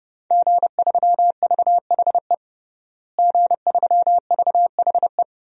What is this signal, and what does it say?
Morse code, heard as a keyed tone, about 30 wpm, 700 Hz. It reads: G3VHE G3VHE
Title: DXCC-100 Call Signs - 5 Characters - Repeated 2x - 30wpm